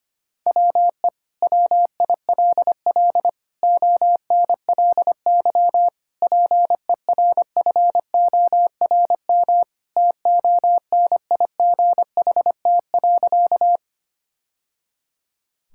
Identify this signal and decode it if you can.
Morse code, heard as a keyed tone, about 25 wpm, 700 Hz. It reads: WE WILL ONLY PERFORM TONIGHT.
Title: Sentences from Top 1000 Words - Review Part 5 - 25wpm